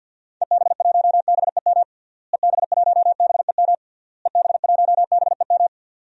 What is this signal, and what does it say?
Morse code, heard as a keyed tone, about 50 wpm, 700 Hz. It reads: EB1BEM EB1BEM EB1BEM
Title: DXCC-100 Call Signs - 6 Characters - Repeated 3x - 50wpm